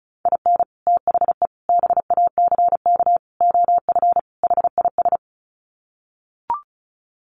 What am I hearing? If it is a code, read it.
Morse code, heard as a keyed tone, about 34 wpm, 700 Hz. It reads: IN THE BACK OF HIS
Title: Sets of 5 Words Encore2 35wpm